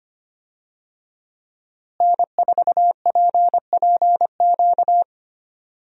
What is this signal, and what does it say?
Morse code, heard as a keyed tone, about 25 wpm, 700 Hz. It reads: N4PPQ
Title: USA Call Signs - 5 Characters - Repeated 3x - 25wpm